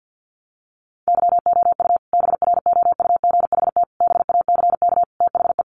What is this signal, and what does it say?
Morse code, heard as a keyed tone, about 50 wpm, 700 Hz. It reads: YOU BROUGHT BACK THE
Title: Sentences from Top 1300 Words 50wpm